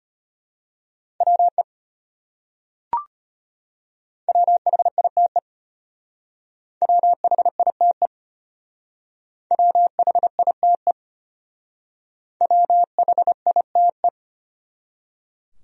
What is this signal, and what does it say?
Morse code, heard as a keyed tone, about 30 wpm, 700 Hz. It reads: WE WHITE WHITE WHITE WHITE
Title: Speed-Racing Top 300 Words 25wpm